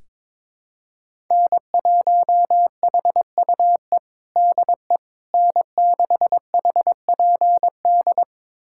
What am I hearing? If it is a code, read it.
Morse code, heard as a keyed tone, about 21 wpm, 700 Hz. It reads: N1HUE DE N6HPD
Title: USA Callsign DE Callsign 22wpm